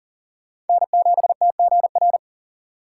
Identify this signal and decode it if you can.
Morse code, heard as a keyed tone, about 52 wpm, 700 Hz. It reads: N7TGR